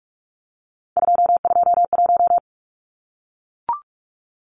Morse code, heard as a keyed tone, about 44 wpm, 700 Hz. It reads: 221
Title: Numbers - 3-Digit 45wpm